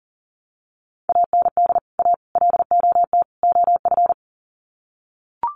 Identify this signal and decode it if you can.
Morse code, heard as a keyed tone, about 40 wpm, 700 Hz. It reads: AND A LOT OF
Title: Sets of 4 Words 40wpm